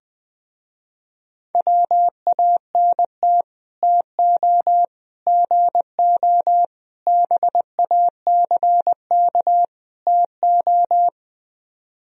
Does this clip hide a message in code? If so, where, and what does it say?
Yes: Morse code from 1.6 to 11.1 s, heard as a keyed tone, about 20 wpm, 700 Hz. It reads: WANT TO GO BACK TO